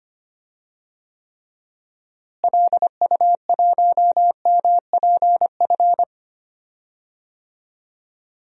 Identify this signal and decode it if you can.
Morse code, heard as a keyed tone, about 25 wpm, 700 Hz. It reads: LU1MPF
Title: DXCC-100 Call Signs - 6 Characters - Repeated 2x - 25wpm